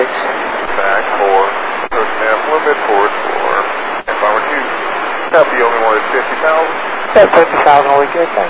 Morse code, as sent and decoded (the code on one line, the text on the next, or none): none